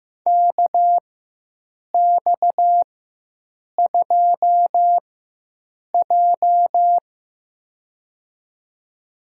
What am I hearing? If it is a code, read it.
Morse code, heard as a keyed tone, about 15 wpm, spaced as for 7 wpm, 700 Hz. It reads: KX2J
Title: USA Call Signs - 4 Characters - 4x Character Spacing - 15wpm